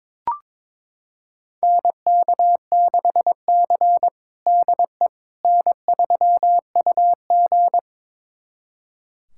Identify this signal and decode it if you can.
Morse code, heard as a keyed tone, about 22 wpm, 700 Hz. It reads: NK6C DE N3UG
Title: USA Callsign DE Callsign - 4 Characters - 22wpm